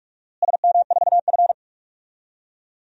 Morse code, heard as a keyed tone, about 42 wpm, 700 Hz. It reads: SM4F
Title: DXCC-100 Call Signs - 4 Characters - Repeated 2x - 45wpm